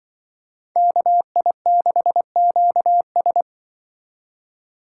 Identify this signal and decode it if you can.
Morse code, heard as a keyed tone, about 25 wpm, 700 Hz. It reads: KI6QS
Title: Speed Racing - USA 2x2 Call Signs 20wpm